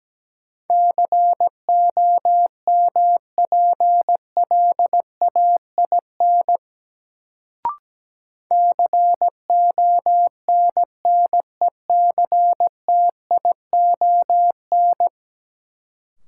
Morse code, heard as a keyed tone, about 17 wpm, 700 Hz. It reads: COMPLAIN CONNECTION
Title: Sentences from Top 1300 Words - New Word Drill - 17wpm